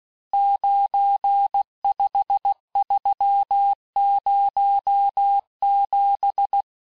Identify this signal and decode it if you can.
Morse code, heard as a keyed tone, about 16 wpm, 800 Hz. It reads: 95307
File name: cours-du-lundi files / Cours-1586-texte-chiffre-vitesse-15mpm